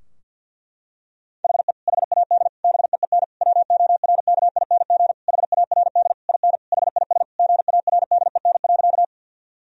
Morse code, heard as a keyed tone, about 51 wpm, 700 Hz. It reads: HE HAD BEEN WORKING HARD IN HIS GARDEN.
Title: Sentences from Top 1000 Words - Review Part 4 - 50wpm